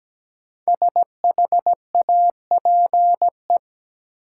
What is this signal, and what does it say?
Morse code, heard as a keyed tone, about 17 wpm, 700 Hz. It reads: SHAPE
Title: Sets of 5 Words Encore - New Word Drill - 17wpm